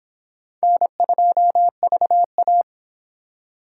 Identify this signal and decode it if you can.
Morse code, heard as a keyed tone, about 26 wpm, 700 Hz. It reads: N2VA